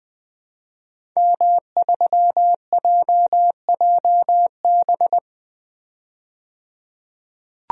Morse code, heard as a keyed tone, about 20 wpm, 700 Hz. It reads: M3JJB
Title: DXCC-100 Call Signs - 5 Characters - 20wpm